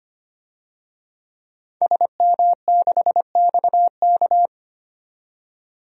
Morse code, heard as a keyed tone, about 25 wpm, 700 Hz. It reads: SM6XK
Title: DXCC-100 Call Signs - 5 Characters - 25wpm